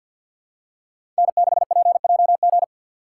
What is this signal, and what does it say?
Morse code, heard as a keyed tone, about 52 wpm, 700 Hz. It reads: N6PJG
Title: USA Call Signs - 5 Characters - 4x Character Spacing - 50wpm